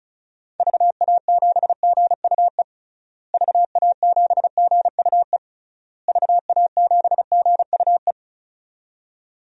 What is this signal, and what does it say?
Morse code, heard as a keyed tone, about 36 wpm, 700 Hz. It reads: VA7GUE VA7GUE VA7GUE
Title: DXCC-100 Call Signs - 6 Characters - Repeated 3x - 35wpm